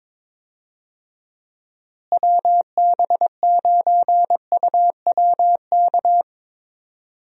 Morse code, heard as a keyed tone, about 22 wpm, 700 Hz. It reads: WB9UWK
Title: USA Call Signs - 6 Characters - Repeated 2x - 22wpm